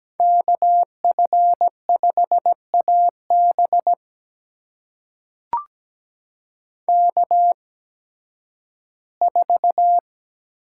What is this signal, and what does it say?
Morse code, heard as a keyed tone, about 17 wpm, spaced as for 11 wpm, 700 Hz. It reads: KF5AB K4
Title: USA Call Signs - 8x Character Spacing - 17wpm